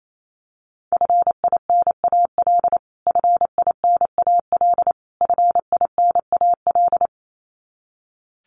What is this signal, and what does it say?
Morse code, heard as a keyed tone, about 28 wpm, 700 Hz. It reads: FINAL FINAL FINAL
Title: Top 500 Words Repeated 28wpm